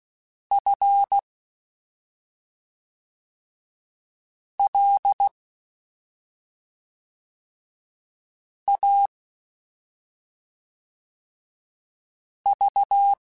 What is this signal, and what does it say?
Morse code, heard as a keyed tone, about 16 wpm, 800 Hz. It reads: F L A V
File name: Special INGs / Special INGs-mp3s / SlowToFaster ing @5